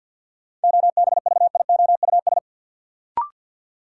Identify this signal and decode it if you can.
Morse code, heard as a keyed tone, about 51 wpm, 700 Hz. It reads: OBVIOUS
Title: Sentences from Top 1600 Words - New Word Drill Alt - 50wpm